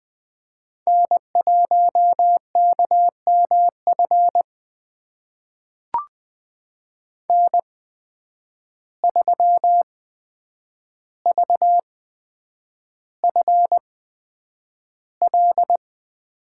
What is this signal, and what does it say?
Morse code, heard as a keyed tone, about 20 wpm, spaced as for 9 wpm, 700 Hz. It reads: N1KMF N3VFL
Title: USA Call Signs - 5 Characters - 8x Character Spacing - 20wpm